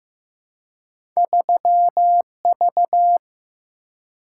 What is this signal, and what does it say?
Morse code, heard as a keyed tone, about 15 wpm, 700 Hz. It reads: 3V
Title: Single-Letter Single-Number 15wpm